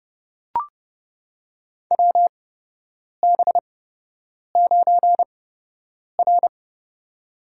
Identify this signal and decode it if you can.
Morse code, heard as a keyed tone, about 30 wpm, spaced as for 8 wpm, 700 Hz. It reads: WB9R
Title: USA Call Signs - 4 Characters - 8x Character Spacing - 30wpm